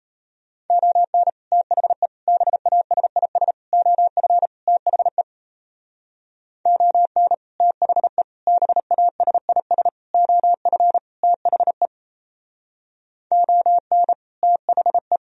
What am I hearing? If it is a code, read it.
Morse code, heard as a keyed tone, about 36 wpm, 700 Hz. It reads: ON THE BASIS OF THE ON THE BASIS OF THE ON THE